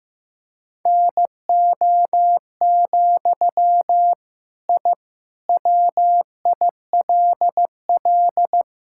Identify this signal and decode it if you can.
Morse code, heard as a keyed tone, about 16 wpm, 700 Hz. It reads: NO, I WILL
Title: Sentences from Top 1000 Words - Review Part 6 - 15wpm